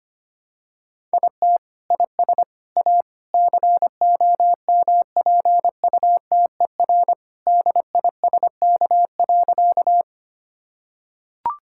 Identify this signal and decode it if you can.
Morse code, heard as a keyed tone, about 25 wpm, 700 Hz. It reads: IT IS A COMPUTER DISK.
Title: Sentences from Top 1000 Words - Review Part 6 - 25wpm